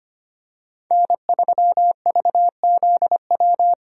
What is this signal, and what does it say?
Morse code, heard as a keyed tone, about 25 wpm, 690 Hz. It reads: N3VZW
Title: USA Call Signs - 5 Characters - 25wpm